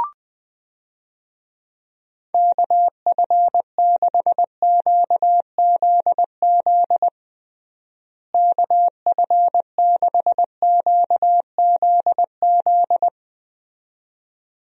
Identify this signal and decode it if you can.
Morse code, heard as a keyed tone, about 20 wpm, 700 Hz. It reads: KF6QZZ KF6QZZ
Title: USA Call Signs - 6 Characters - Repeated 2x - 20wpm